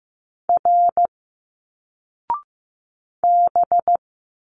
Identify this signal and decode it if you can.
Morse code, heard as a keyed tone, about 15 wpm, 700 Hz. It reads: R B
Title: Single Letters 15wpm